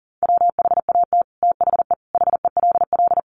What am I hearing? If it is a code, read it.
Morse code, heard as a keyed tone, about 40 wpm, 700 Hz. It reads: WHAT THE HELL
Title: Sets of 5 Words Encore 40wpm